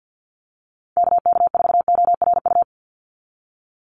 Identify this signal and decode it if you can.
Morse code, heard as a keyed tone, about 48 wpm, 700 Hz. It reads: KK4ORU
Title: USA Call Signs 50wpm